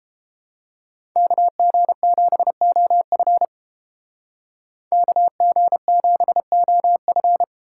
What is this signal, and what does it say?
Morse code, heard as a keyed tone, about 30 wpm, 700 Hz. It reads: KG7OF KG7OF